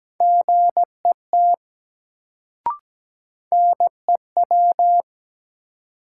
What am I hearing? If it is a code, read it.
Morse code, heard as a keyed tone, about 17 wpm, 700 Hz. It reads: GET NEW
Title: Sets of 2 Words - New Word Drill - 17wpm